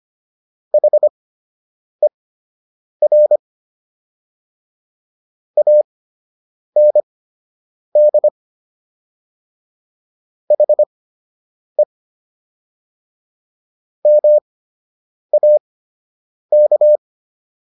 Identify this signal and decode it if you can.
Morse code, heard as a keyed tone, about 25 wpm, spaced as for 8 wpm, 600 Hz. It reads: HER AND HE MAK
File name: WD104 08